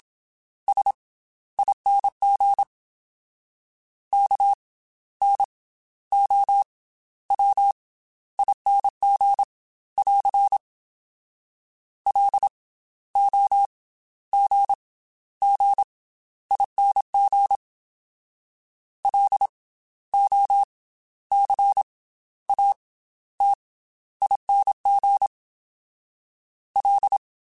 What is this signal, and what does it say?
Morse code, heard as a keyed tone, about 27 wpm, spaced as for 13 wpm, 800 Hz. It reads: SING KNOWING <AR> LOGGING LOCATING L
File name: Special INGs / Special INGs-mp3s / Slow ing @25